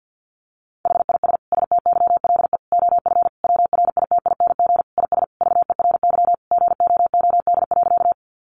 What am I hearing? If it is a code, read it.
Morse code, heard as a keyed tone, about 50 wpm, 700 Hz. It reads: HIS STYLE OF WRITING IS VERY GOOD.
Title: Sentences from Top 800 Words 50wpm